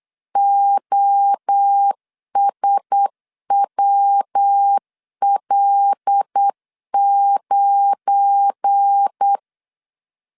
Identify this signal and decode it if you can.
Morse code, heard as a keyed tone, about 8 wpm, 800 Hz. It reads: OSWL9
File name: LEZIONI / 7 GF9Z / 7 GF9 8wpm 3545